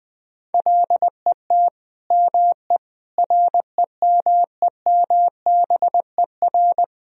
Morse code, heard as a keyed tone, about 20 wpm, 700 Hz. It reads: LET ME REMEMBER
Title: Sentences from Top 400 Words and Abbreviations 20wpm